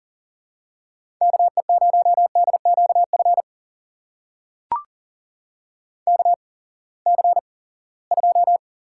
Morse code, heard as a keyed tone, about 38 wpm, spaced as for 22 wpm, 700 Hz. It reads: KE0DQF KC2
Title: USA Call Signs - 6 Characters - 8x Character Spacing - 40wpm